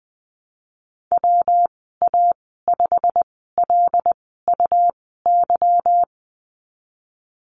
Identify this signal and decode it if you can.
Morse code, heard as a keyed tone, about 19 wpm, spaced as for 14 wpm, 700 Hz. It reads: WA5LUY
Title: 179 Lesson 29g - Slash - Call Signs 20wpm